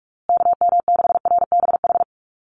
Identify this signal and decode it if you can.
Morse code, heard as a keyed tone, about 44 wpm, 700 Hz. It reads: KM6RBH
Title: USA Call Signs Repeated 3x 45wpm